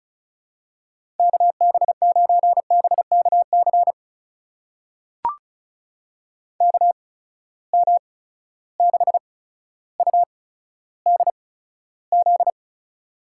Morse code, heard as a keyed tone, about 36 wpm, spaced as for 15 wpm, 700 Hz. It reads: KB9BKC KM6UDZ
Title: USA Call Signs - 6 Characters - 8x Character Spacing - 35wpm